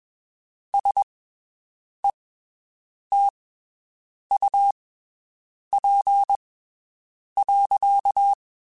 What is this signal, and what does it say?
Morse code, heard as a keyed tone, about 21 wpm, spaced as for 7 wpm, 800 Hz. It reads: SETUP.
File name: (spaced) / QSOs Spaced-mp3s / WD0BGP 1 @20 sp